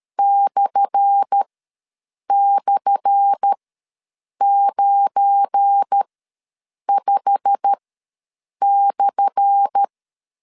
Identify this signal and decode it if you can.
Morse code, heard as a keyed tone, about 13 wpm, spaced as for 7 wpm, 800 Hz. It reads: //95/
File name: LEZIONI / 10 SIGNS / 10A SLASH ws 12wpm 3443